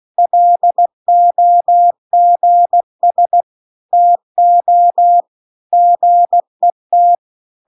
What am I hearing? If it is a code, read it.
Morse code, heard as a keyed tone, about 16 wpm, 700 Hz. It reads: LOGS TO GET